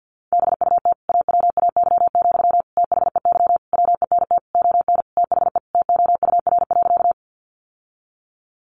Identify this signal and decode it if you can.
Morse code, heard as a keyed tone, about 50 wpm, 700 Hz. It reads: BUT AWAY, THEY WENT ON THE TOUR.
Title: Sentences from Top 1400 Words 50wpm